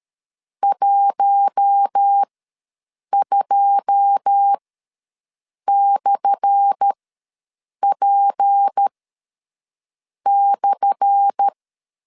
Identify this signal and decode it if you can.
Morse code, heard as a keyed tone, about 13 wpm, spaced as for 6 wpm, 800 Hz. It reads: 12/P/
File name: LEZIONI / 10 SIGNS / 10A SLASH ws 12wpm 3443